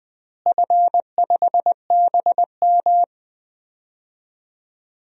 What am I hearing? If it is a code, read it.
Morse code, heard as a keyed tone, about 20 wpm, 700 Hz. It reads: F5BM
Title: DXCC-100 Call Signs - 4 Characters - Repeated 3x - 20wpm